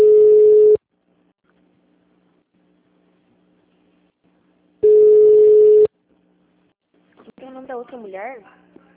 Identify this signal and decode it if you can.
Ringback tone (Europe, 425 Hz), with speech.
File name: out- -2003- - - .666